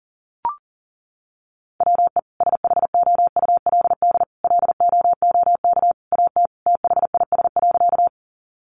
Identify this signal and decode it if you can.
Morse code, heard as a keyed tone, about 40 wpm, 700 Hz. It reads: WE SHOULD LOOK AT THIS.
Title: Sentences from Top 300 Words 40wpm